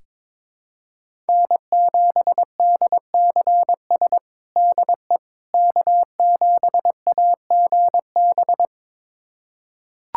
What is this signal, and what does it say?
Morse code, heard as a keyed tone, about 22 wpm, 700 Hz. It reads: N7DCS DE K7AGB
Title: USA Callsign DE Callsign 22wpm